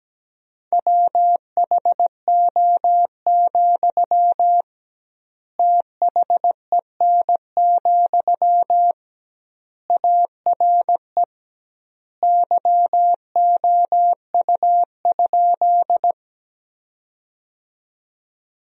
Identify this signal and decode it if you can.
Morse code, heard as a keyed tone, about 17 wpm, 700 Hz. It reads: WHO, THEN, ARE YOU?
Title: 2x Word-Spacing - Sentences from Top 100 Words - 17wpm